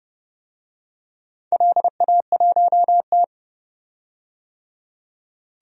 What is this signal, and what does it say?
Morse code, heard as a keyed tone, about 30 wpm, 700 Hz. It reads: LA1T
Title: DXCC-100 Call Signs - 4 Characters - 30wpm